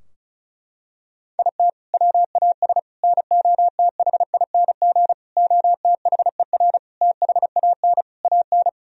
Morse code, heard as a keyed tone, about 34 wpm, 700 Hz. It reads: IT WAS NOTHING OTHER THAN AN
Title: Sentences from Top 1000 Words - Review Part 2 - 35wpm